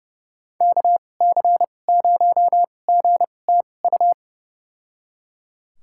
Morse code, heard as a keyed tone, about 30 wpm, spaced as for 22 wpm, 700 Hz. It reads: KC0GTU
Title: USA Call Signs - 6 Characters - 2x Character Spacing - 30wpm